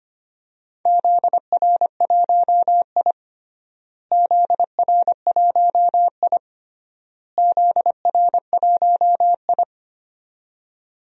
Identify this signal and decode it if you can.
Morse code, heard as a keyed tone, about 25 wpm, 700 Hz. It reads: ZR1I ZR1I ZR1I
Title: DXCC-100 Call Signs - 4 Characters - Repeated 3x - 25wpm